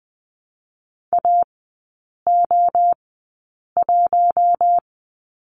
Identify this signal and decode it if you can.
Morse code, heard as a keyed tone, about 20 wpm, spaced as for 8 wpm, 700 Hz. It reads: AO1
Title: 041 Lesson 8e - Number 1 - Three Characters 20wpm